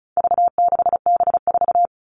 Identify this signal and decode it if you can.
Morse code, heard as a keyed tone, about 34 wpm, 700 Hz. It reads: V6B4
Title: All Cut Numbers - 4-Digit 35wpm